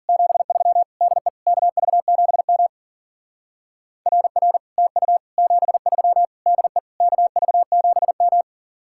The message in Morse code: 73 DE KV7M RR TU 73 DE KV7M

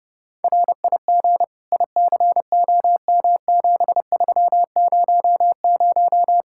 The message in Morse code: RIG ICOM7300

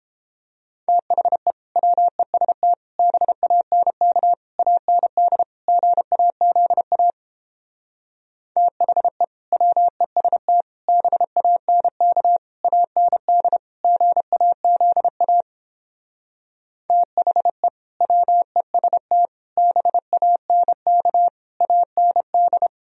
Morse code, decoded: THE WEST BANK AND GAZA THE WEST BANK AND GAZA THE WEST BANK AND

